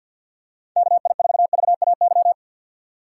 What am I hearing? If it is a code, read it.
Morse code, heard as a keyed tone, about 52 wpm, 700 Hz. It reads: KI4VAY
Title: USA Call Signs - 6 Characters - 3x Character Spacing - 50wpm